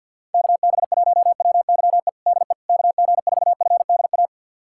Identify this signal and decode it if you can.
Morse code, heard as a keyed tone, about 51 wpm, 700 Hz. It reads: KB1WYE DE KG4FDA
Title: USA Callsign DE Callsign - 6 Characters - 50wpm